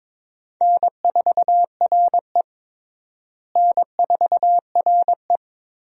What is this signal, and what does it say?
Morse code, heard as a keyed tone, about 22 wpm, 700 Hz. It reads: N4RE N4RE
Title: USA Call Signs - 4 Characters - Repeated 3x - 22wpm